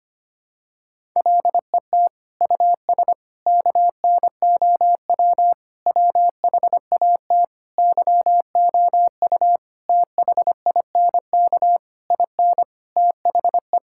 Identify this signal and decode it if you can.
Morse code, heard as a keyed tone, about 25 wpm, 700 Hz. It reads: LET US KNOW WHAT YOU THINK IN THE
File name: Sets of 9 Words - 25wpm